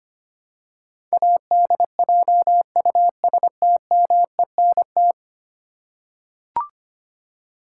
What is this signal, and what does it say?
Morse code, heard as a keyed tone, about 25 wpm, 700 Hz. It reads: ADJUSTMENT
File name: 10 Letter Words - First 3 Letters Given - 25wpm